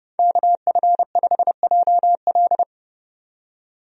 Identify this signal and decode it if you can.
Morse code, heard as a keyed tone, about 31 wpm, 700 Hz. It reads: KF5JL
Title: USA Call Signs - 5 Characters - 30wpm